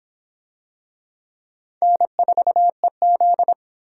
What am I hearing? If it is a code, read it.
Morse code, heard as a keyed tone, about 26 wpm, 700 Hz. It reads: N4EZ